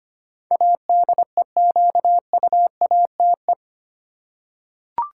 Morse code, heard as a keyed tone, about 25 wpm, 700 Hz. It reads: ADEQUATE
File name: Eight Letter Words - 25wpm